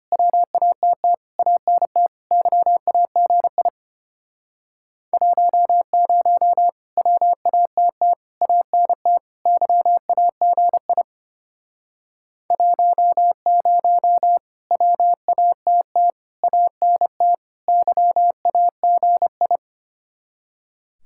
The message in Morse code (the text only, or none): WATT ANT YAGI 10 WATT ANT YAGI 10 WATT ANT YAGI